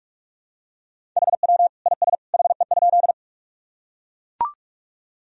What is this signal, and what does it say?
Morse code, heard as a keyed tone, about 43 wpm, 700 Hz. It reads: HW IS HE?
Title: Sentences from Top 100 Words and Abbreviations 45wpm